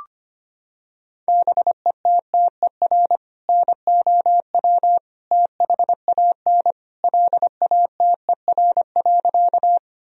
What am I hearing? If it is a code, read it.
Morse code, heard as a keyed tone, about 25 wpm, 700 Hz. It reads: BETTER NOW THAN LATER.
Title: Sentences from Top 1000 Words - Review Part 4 - 25wpm